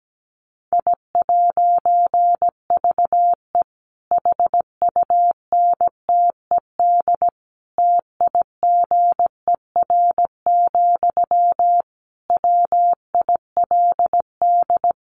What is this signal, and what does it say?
Morse code, heard as a keyed tone, about 17 wpm, 700 Hz. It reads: I'VE HUNTED TIGER, WILD
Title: A Sound Of Thunder 17wpm